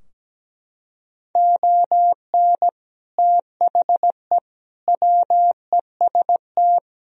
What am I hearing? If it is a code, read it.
Morse code, heard as a keyed tone, about 17 wpm, 700 Hz. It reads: ON THE WEST
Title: Sentences from Top 1000 Words - Review Part 4 - 17wpm